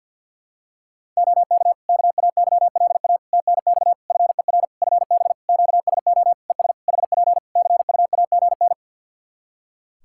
Morse code, heard as a keyed tone, about 50 wpm, 700 Hz. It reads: OK KAYLA TNX FER FB QSO ES HP CUAGN
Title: Practice QSO - Exchange 4 - Ending - 50wpm